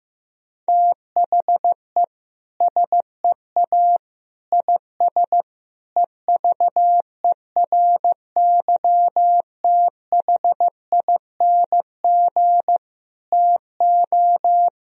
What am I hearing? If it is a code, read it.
Morse code, heard as a keyed tone, about 15 wpm, 700 Hz. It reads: THE SEA IS EVERYTHING TO
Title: Sentences from Top 1000 Words - Review Part 1 - 15wpm